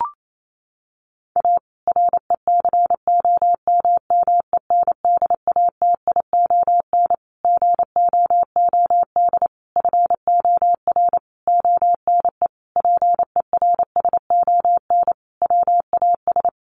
Morse code, heard as a keyed tone, about 28 wpm, 700 Hz. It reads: A RECOMMENDATION GOOD FOR ONE PERSON WAS
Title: Sentences from Top 1400 Words 28wpm